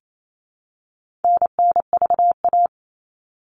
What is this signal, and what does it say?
Morse code, heard as a keyed tone, about 28 wpm, 700 Hz. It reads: NNVA
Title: All Cut Numbers - 4-Digit 28wpm